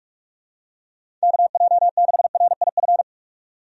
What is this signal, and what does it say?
Morse code, heard as a keyed tone, about 46 wpm, 700 Hz. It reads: KJ6RIF
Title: USA Call Signs - 6 Characters - 45wpm